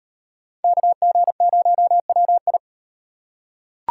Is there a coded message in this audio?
Yes: Morse code from 0.6 to 2.6 s, heard as a keyed tone, about 38 wpm, 700 Hz. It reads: KG0WI